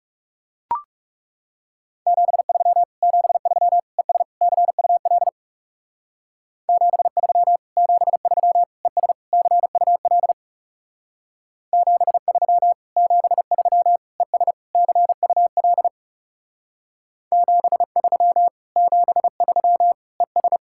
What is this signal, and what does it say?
Morse code, heard as a keyed tone, about 40 wpm, 700 Hz. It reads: 73 73 ES CUL 73 73 ES CUL 73 73 ES CUL 73 73 ES